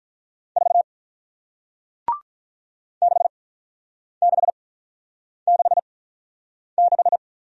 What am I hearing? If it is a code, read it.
Morse code, heard as a keyed tone, about 48 wpm, 700 Hz. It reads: 4 6 6 6 6